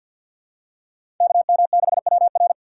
Morse code, heard as a keyed tone, about 52 wpm, 700 Hz. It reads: KM6WR
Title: USA Call Signs - 5 Characters - 6x Character Spacing - 50wpm